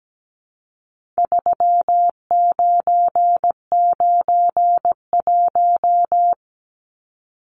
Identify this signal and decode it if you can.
Morse code, heard as a keyed tone, about 17 wpm, 700 Hz. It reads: 3991
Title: Numbers - 4-Digit 17wpm